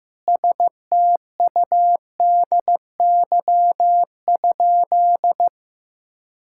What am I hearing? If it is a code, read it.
Morse code, heard as a keyed tone, about 15 wpm, 700 Hz. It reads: STUDY?
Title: Sentences from Top 1000 Words - Review Part 6 - 15wpm